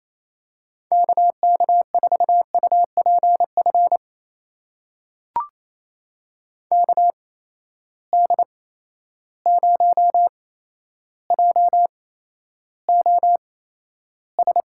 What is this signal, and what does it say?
Morse code, heard as a keyed tone, about 27 wpm, spaced as for 12 wpm, 700 Hz. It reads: KK4UPF KD0JOS
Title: USA Call Signs - 6 Characters - 8x Character Spacing - 28wpm